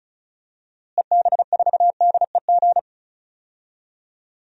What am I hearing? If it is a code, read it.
Morse code, heard as a keyed tone, about 36 wpm, 700 Hz. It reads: EB4DEG
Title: DXCC-100 Call Signs - 6 Characters - Repeated 2x - 35wpm